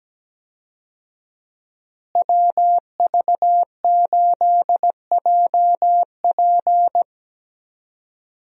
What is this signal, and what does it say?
Morse code, heard as a keyed tone, about 17 wpm, 700 Hz. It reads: WV8JP
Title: USA Call Signs - 5 Characters - Repeated 3x - 17wpm